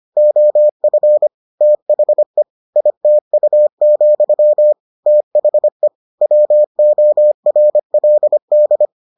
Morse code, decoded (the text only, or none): OF THE ITU, THE WORLD